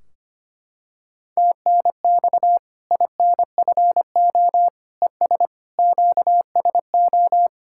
Morse code, heard as a keyed tone, about 25 wpm, 700 Hz. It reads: TNX INFO ES QSO
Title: Practice QSO - Exchange 4 - Ending - 25wpm